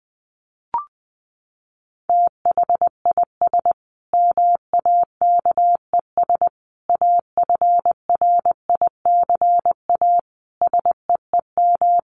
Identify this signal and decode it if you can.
Morse code, heard as a keyed tone, about 20 wpm, 700 Hz. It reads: THIS MAKES AFRICA SEEM
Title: A Sound Of Thunder 20wpm